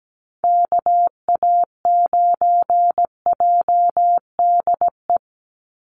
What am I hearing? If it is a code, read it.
Morse code, heard as a keyed tone, about 17 wpm, 700 Hz. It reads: KA9JDE
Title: USA Call Signs 17wpm